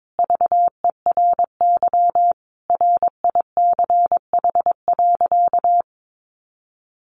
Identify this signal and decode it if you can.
Morse code, heard as a keyed tone, about 22 wpm, 700 Hz. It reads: VERY RICH.
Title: Sentences from Top 1200 Words 22wpm